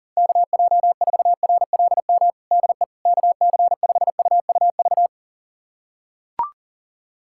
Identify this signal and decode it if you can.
Morse code, heard as a keyed tone, about 40 wpm, 700 Hz. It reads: KJ4RLM DE KC5UUV
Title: USA Callsign DE Callsign - 6 Characters - 40wpm